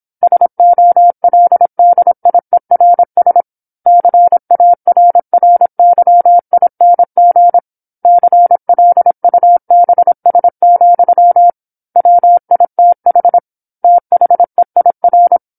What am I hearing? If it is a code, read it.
Morse code, heard as a keyed tone, about 27 wpm, 700 Hz. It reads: SOLDIERS CARRYING CLUBS, WITH THEIR
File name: 08-Chapter-08 26 WPM